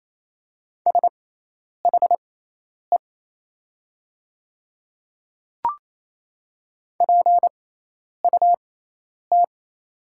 Morse code, heard as a keyed tone, about 27 wpm, spaced as for 10 wpm, 690 Hz. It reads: SHE PUT
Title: 6x Character Spacing - Top 300 Words - 28wpm